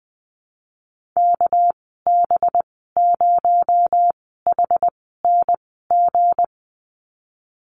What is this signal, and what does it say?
Morse code, heard as a keyed tone, about 20 wpm, spaced as for 14 wpm, 700 Hz. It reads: KB0HNG
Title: 238 Lesson 38g - Letter q - Call Signs 20wpm